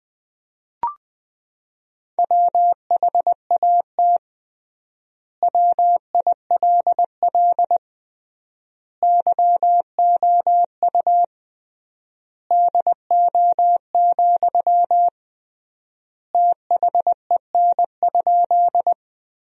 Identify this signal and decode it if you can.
Morse code, heard as a keyed tone, about 20 wpm, 700 Hz. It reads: WHAT WILL YOU DO, THEN?
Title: 3x Word-Spacing - Sentences from Top 100 Words - 20wpm